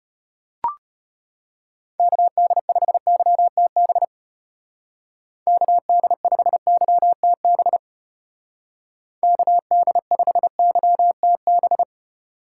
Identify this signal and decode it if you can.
Morse code, heard as a keyed tone, about 34 wpm, 700 Hz. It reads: KD5YTB KD5YTB KD5YTB